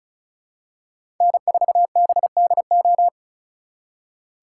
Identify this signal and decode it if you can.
Morse code, heard as a keyed tone, about 36 wpm, 700 Hz. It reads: N4BDO
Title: USA Call Signs - 5 Characters - 2x Character Spacing - 35wpm